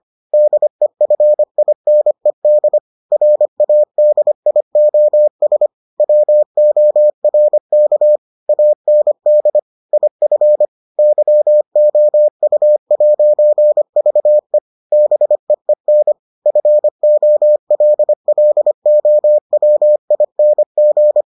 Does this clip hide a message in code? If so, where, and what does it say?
Yes: Morse code from 0.3 to 21.2 s, heard as a keyed tone, about 25 wpm, 600 Hz. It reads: DEFINED RADIOS WORK AND IF YOU'VE BEEN FOLLOWING